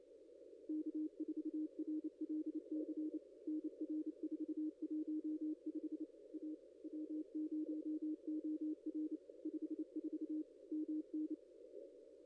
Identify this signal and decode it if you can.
Morse code, heard as a keyed tone, about 28 wpm, 310 Hz. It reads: K4RLC NR 415 A W0OR 54 MN
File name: contests / ss-cw-k5tr-op-n5rz -0900